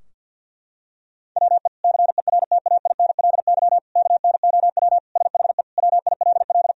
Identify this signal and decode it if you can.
Morse code, heard as a keyed tone, about 50 wpm, 700 Hz. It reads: WE CERTAINLY KNOW SHE WILL
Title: Sentences from Top 1000 Words - Review Part 4 - 50wpm